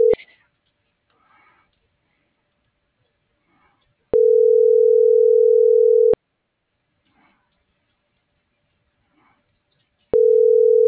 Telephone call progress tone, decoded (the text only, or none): ringback tone